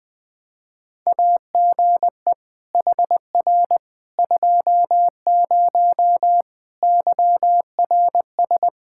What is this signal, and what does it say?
Morse code, heard as a keyed tone, about 20 wpm, 700 Hz. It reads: AGE HR 20 YRS
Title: K1USN SST QSO Practice 20wpm